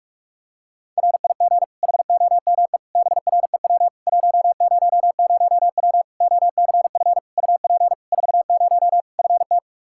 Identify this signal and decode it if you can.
Morse code, heard as a keyed tone, about 46 wpm, 700 Hz. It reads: RIG HOME BREW 100W OCF UP 40 FT
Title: K1USN SST QSO Practice 45wpm